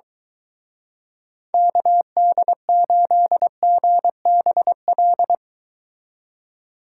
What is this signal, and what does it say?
Morse code, heard as a keyed tone, about 24 wpm, 700 Hz. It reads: KD8GBL